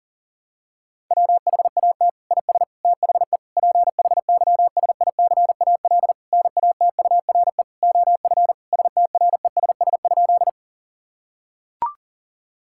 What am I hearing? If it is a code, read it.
Morse code, heard as a keyed tone, about 40 wpm, 700 Hz. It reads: WHAT IS THE PHYSICAL NATURE OF STRESS?
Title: Sentences from Top 1000 Words - Review Part 6 - 40wpm